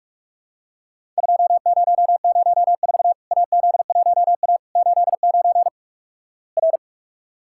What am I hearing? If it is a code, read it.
Morse code, heard as a keyed tone, about 44 wpm, 700 Hz. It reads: 2004 AZ1A 89 R
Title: Worked All Europe DX Contest - QTC Exchange - 45wpm